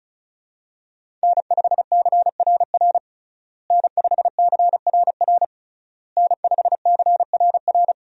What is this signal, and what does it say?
Morse code, heard as a keyed tone, about 34 wpm, 700 Hz. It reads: N5CRR N5CRR N5CRR
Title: USA Call Signs - 5 Characters - Repeated 3x - 35wpm